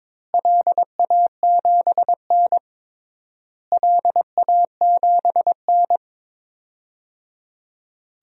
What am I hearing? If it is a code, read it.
Morse code, heard as a keyed tone, about 22 wpm, 700 Hz. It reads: LA7N LA7N
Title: DXCC-100 Call Signs - 4 Characters - Repeated 3x - 22wpm